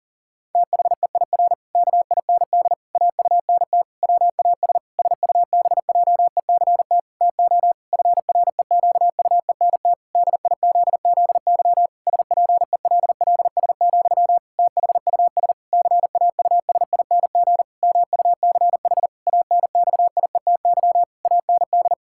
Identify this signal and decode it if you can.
Morse code, heard as a keyed tone, about 40 wpm, 700 Hz. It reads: THEIR KIND AUNT WAS SUBJECT TO FREQUENT DIZZY SPELLS, THUS CAUSING MUCH ANXIETY AND
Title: Pangram Challenge v2 - 40wpm